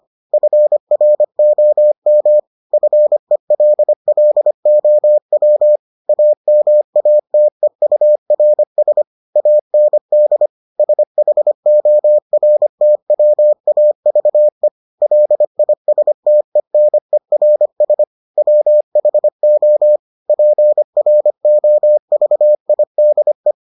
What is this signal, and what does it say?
Morse code, heard as a keyed tone, about 25 wpm, 600 Hz. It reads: FROM FELLOW AMATEURS AND SHORTWAVE LISTENERS WHO PROVIDE